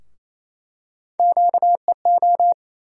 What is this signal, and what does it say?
Morse code, heard as a keyed tone, about 28 wpm, 700 Hz. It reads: QEO
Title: USA Call Sign Postfixes - 3 Letters 28wpm